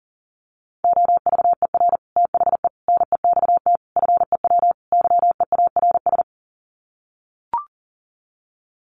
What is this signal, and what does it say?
Morse code, heard as a keyed tone, about 40 wpm, 700 Hz. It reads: OVER THE NEXT FEW YEARS
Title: Sets of 5 Words Encore 40wpm